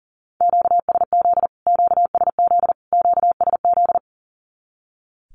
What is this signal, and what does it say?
Morse code, heard as a keyed tone, about 38 wpm, 700 Hz. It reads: QSZ QSZ QSZ
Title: Q-Codes Repeated 3x 40wpm